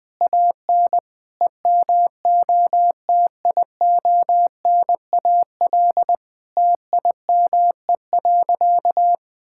Morse code, heard as a keyed tone, about 20 wpm, 700 Hz. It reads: AN EMOTIONAL TIME.
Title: Sentences from Top 1000 Words - Review Part 5 - 20wpm